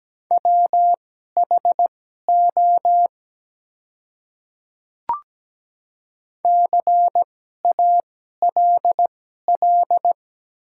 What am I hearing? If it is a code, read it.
Morse code, heard as a keyed tone, about 17 wpm, spaced as for 12 wpm, 700 Hz. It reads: WHO CALL
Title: 2x Character Spacing - Top 100 Words - 17wpm